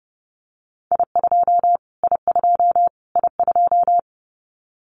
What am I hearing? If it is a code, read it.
Morse code, heard as a keyed tone, about 30 wpm, 700 Hz. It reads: I2 I2 I2
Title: Callsign Prefixes for Top 25 DXCC - Repeated 3x - 30wpm